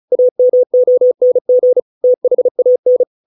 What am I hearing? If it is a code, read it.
Morse code, heard as a keyed tone, about 34 wpm, 500 Hz. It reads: AMONG THAN